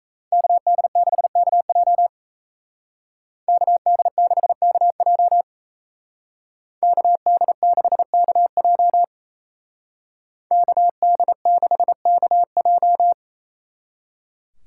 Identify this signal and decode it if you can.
Morse code, heard as a keyed tone, about 36 wpm, 700 Hz. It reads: KD6KJ KD6KJ KD6KJ KD6KJ